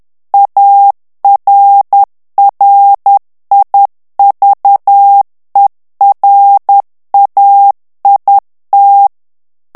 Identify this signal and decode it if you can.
Morse code, heard as a keyed tone, about 10 wpm, 800 Hz. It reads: ARRIVERAIT
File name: morse mp3 / FRANCAIS TEXTE CLAIR 2 VITESSE 10 MOTS MINUTES